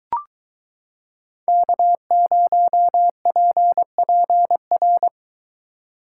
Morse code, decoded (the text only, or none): K0PPR